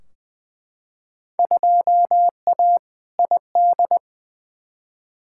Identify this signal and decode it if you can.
Morse code, heard as a keyed tone, about 20 wpm, 700 Hz. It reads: 2A ID
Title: Field Day Exchanges Encore - 20wpm